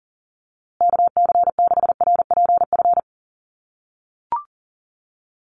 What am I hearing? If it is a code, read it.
Morse code, heard as a keyed tone, about 40 wpm, 700 Hz. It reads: KC6RPF
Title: USA Call Signs Repeated 3x 40wpm